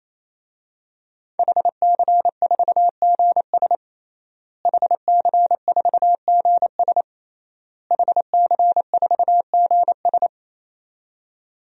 Morse code, heard as a keyed tone, about 28 wpm, 700 Hz. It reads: HC4GS HC4GS HC4GS
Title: DXCC-100 Call Signs - 5 Characters - Repeated 3x - 28wpm